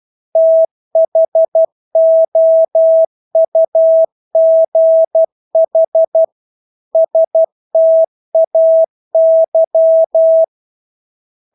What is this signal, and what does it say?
Morse code, heard as a keyed tone, about 12 wpm, 650 Hz. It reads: THOUGH STAY
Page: Engelske ord 12 wpm